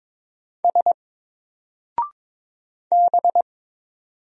Morse code, heard as a keyed tone, about 22 wpm, 700 Hz. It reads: S B
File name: Single-Letter-Number-22wpm